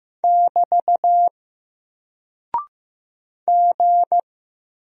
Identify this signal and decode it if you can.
Morse code, heard as a keyed tone, about 15 wpm, 700 Hz. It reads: <BT> G
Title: Single Character (Letter:Number:Punctuation:Prosign) Alt 15wpm